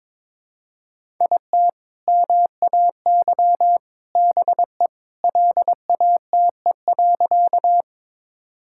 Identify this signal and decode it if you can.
Morse code, heard as a keyed tone, about 22 wpm, 700 Hz. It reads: IT MAY BE LATE.
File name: practice / 22wpm / Sentences with 3-Word Phrases - 22wpm